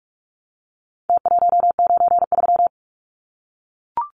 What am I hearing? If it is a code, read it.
Morse code, heard as a keyed tone, about 44 wpm, 700 Hz. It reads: T193
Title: Numbers - Contest Style - 4-Digit 45wpm